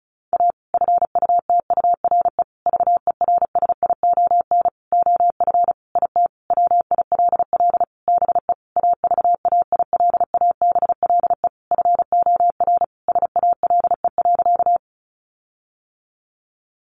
Morse code, heard as a keyed tone, about 34 wpm, 700 Hz. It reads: A FUTURE VERSION OF IT WILL BE AVAILABLE FOR SALE.
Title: Sentences from Top 1400 Words 35wpm